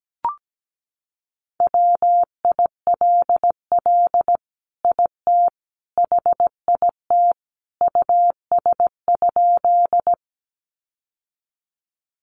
Morse code, heard as a keyed tone, about 17 wpm, 700 Hz. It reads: WILL IT HIT US?
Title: Sentences from Top 1200 Words 17wpm